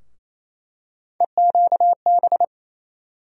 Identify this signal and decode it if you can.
Morse code, heard as a keyed tone, about 28 wpm, 700 Hz. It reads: EQB
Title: USA Call Sign Postfixes - 3 Letters 28wpm